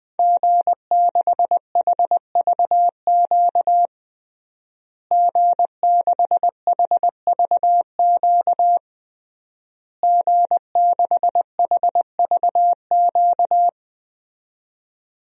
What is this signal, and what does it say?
Morse code, heard as a keyed tone, about 21 wpm, 700 Hz. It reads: G6HVQ G6HVQ G6HVQ
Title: DXCC-100 Call Signs - 5 Characters - Repeated 3x - 20wpm